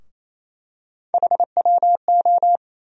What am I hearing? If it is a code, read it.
Morse code, heard as a keyed tone, about 28 wpm, 700 Hz. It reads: HWO
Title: USA Call Sign Postfixes - 3 Letters 28wpm